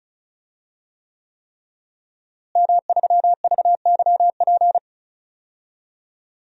Morse code, heard as a keyed tone, about 36 wpm, 700 Hz. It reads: M3VYP
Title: DXCC-100 Call Signs - 5 Characters - Repeated 3x - 35wpm